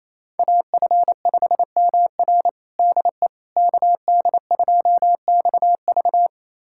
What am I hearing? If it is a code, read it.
Morse code, heard as a keyed tone, about 28 wpm, 700 Hz. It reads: AF5MR DE KD2XV
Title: USA Callsign DE Callsign - 5 Characters - 28wpm